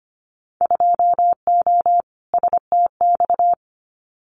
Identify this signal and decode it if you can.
Morse code, heard as a keyed tone, about 25 wpm, 700 Hz. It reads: 2O STX
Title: Winter Field Day Exchanges 25wpm